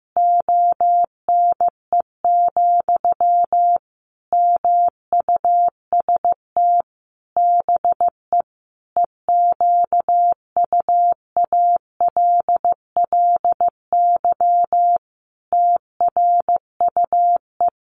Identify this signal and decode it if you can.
Morse code, heard as a keyed tone, about 14 wpm, 700 Hz. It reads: ONE, MUST BE EQUALLY TRUE
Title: Sentences from Top 1400 Words 15wpm